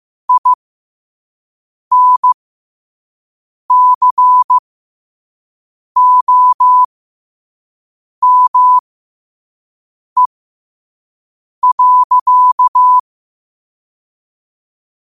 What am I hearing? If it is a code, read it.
Morse code, heard as a keyed tone, about 15 wpm, spaced as for 5 wpm, 1000 Hz. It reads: INCOME.
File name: qotdfiles / 5wpm